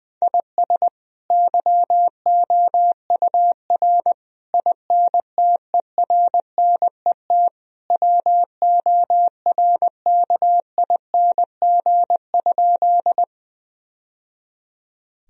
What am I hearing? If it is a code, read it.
Morse code, heard as a keyed tone, about 20 wpm, 700 Hz. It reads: IS YOUR INTERNET WORKING?
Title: Sentences from Top 1000 Words - Review Part 4 - 20wpm